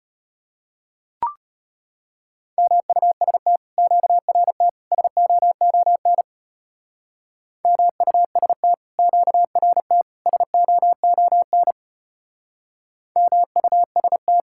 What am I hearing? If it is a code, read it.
Morse code, heard as a keyed tone, about 32 wpm, 700 Hz. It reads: MUST QRT SOON MUST QRT SOON MUST